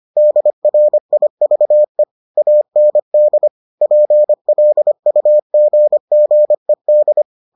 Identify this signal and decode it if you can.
Morse code, heard as a keyed tone, about 25 wpm, 600 Hz. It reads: DRIVE AND PLUGGED